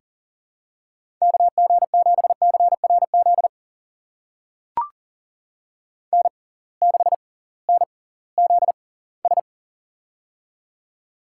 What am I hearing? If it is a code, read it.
Morse code, heard as a keyed tone, about 40 wpm, spaced as for 22 wpm, 700 Hz. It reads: KG7CRZ N6NZS
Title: USA Call Signs - 6x Character Spacing - 40wpm